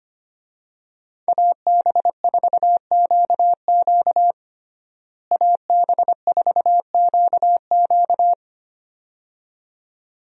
Morse code, heard as a keyed tone, about 25 wpm, 700 Hz. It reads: AB4QQ AB4QQ
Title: USA Call Signs - 5 Characters - Repeated 2x - 25wpm